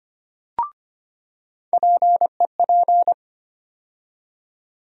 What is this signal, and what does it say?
Morse code, heard as a keyed tone, about 25 wpm, 700 Hz. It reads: PEP
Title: USA Call Sign Postfixes - 3 Letters 25wpm